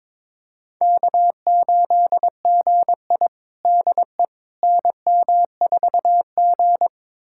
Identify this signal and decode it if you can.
Morse code, heard as a keyed tone, about 22 wpm, 700 Hz. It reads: K8GI DE NM4G
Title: USA Callsign DE Callsign - 4 Characters - 22wpm